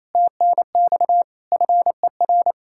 Morse code, heard as a keyed tone, about 37 wpm, 700 Hz. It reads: TNX FER